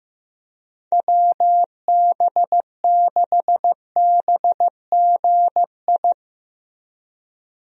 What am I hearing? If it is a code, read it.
Morse code, heard as a keyed tone, about 15 wpm, 700 Hz. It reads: WB6BGI
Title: USA Call Signs - 6 Characters - Repeated 2x - 15wpm